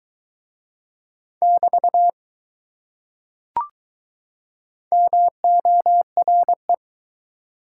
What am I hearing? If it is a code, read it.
Morse code, heard as a keyed tone, about 20 wpm, 700 Hz. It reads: <BT> MORE